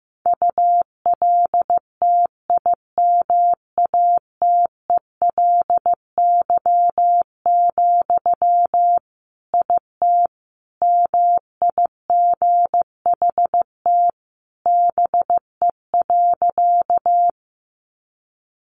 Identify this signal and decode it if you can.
Morse code, heard as a keyed tone, about 15 wpm, 700 Hz. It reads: ULTIMATELY, IT MIGHT BE.
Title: Sentences from Top 1200 Words 15wpm